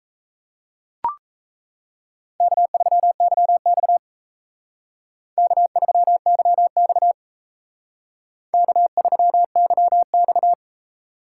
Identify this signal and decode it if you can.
Morse code, heard as a keyed tone, about 33 wpm, 700 Hz. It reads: K3YX K3YX K3YX
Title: Speed Racing - USA 1x2 Call Signs 28wpm